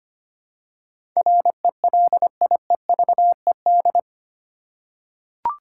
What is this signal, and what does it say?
Morse code, heard as a keyed tone, about 25 wpm, 700 Hz. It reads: RELIEVED
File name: 8 Letter Words - First 3 Letters Given - 25wpm